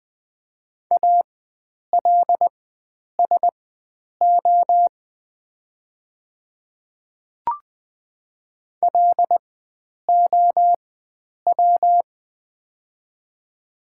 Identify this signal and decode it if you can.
Morse code, heard as a keyed tone, about 20 wpm, spaced as for 9 wpm, 700 Hz. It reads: ALSO LOW
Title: 4x Character Spacing - Top 300 Words - 20wpm